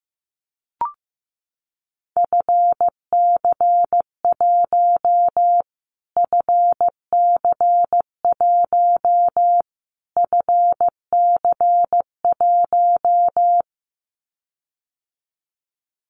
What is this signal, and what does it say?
Morse code, heard as a keyed tone, about 15 wpm, 700 Hz. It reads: FC1 FC1 FC1
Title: Callsign Prefixes for Top 25 DXCC - Repeated 3x - 15wpm